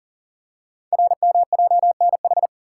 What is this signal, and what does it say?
Morse code, heard as a keyed tone, about 40 wpm, 700 Hz. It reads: RMJNH
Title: Code Groups - 5 Letters - 40wpm